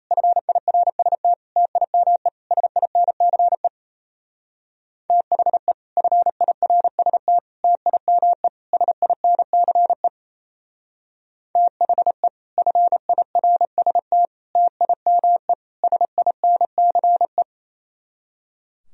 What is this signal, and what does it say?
Morse code, heard as a keyed tone, about 34 wpm, 700 Hz. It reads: FIRST TIME SINCE THE FIRST TIME SINCE THE FIRST TIME SINCE
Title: Speed-Racing - Sets of 4 Words 28wpm